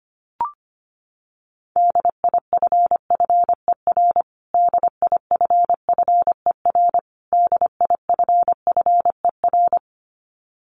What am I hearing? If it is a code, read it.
Morse code, heard as a keyed tone, about 25 wpm, 700 Hz. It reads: DIFFER DIFFER DIFFER
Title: Top 200 Words Repeated 25wpm